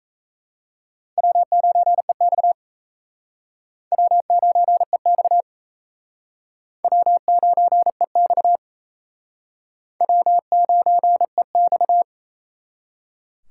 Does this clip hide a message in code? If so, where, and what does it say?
Yes: Morse code from 1.2 to 12.0 s, heard as a keyed tone, about 37 wpm, 700 Hz. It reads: W9EX W9EX W9EX W9EX